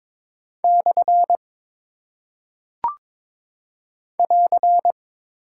Morse code, heard as a keyed tone, about 22 wpm, 700 Hz. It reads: / <AR>
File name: practice / 22wpm / Single Punctuation-Prosign - 22wpm